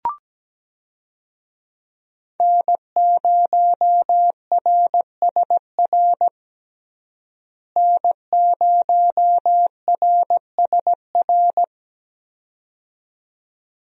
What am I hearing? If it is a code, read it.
Morse code, heard as a keyed tone, about 17 wpm, 700 Hz. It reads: N0RSR N0RSR
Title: USA Call Signs - 5 Characters - Repeated 2x - 17wpm